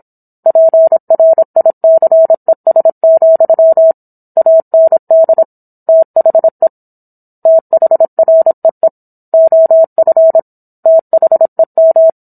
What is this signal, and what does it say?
Morse code, heard as a keyed tone, about 27 wpm, 650 Hz. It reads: PRICES, AND THE THREE OF THEM